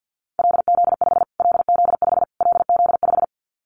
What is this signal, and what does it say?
Morse code, heard as a keyed tone, about 49 wpm, 700 Hz. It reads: LZ5 LZ5 LZ5
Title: Callsign Prefixes for Top 25 DXCC - Repeated 3x - 50wpm